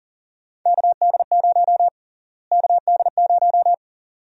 Morse code, heard as a keyed tone, about 40 wpm, 700 Hz. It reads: KD0 KD0
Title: USA Call Sign Prefixes - Repeated 2x - 40wpm